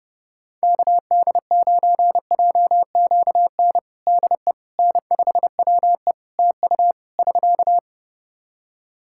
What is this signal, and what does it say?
Morse code, heard as a keyed tone, about 30 wpm, 700 Hz. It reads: KD9JQN DE N5WE TU <SK>
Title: Practice QSO - Exchange 4 - Ending - 30wpm